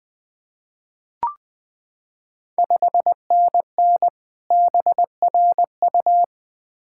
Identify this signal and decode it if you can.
Morse code, heard as a keyed tone, about 20 wpm, 690 Hz. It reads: 5NN BRU
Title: Ontario QSO Party Exchange 20wpm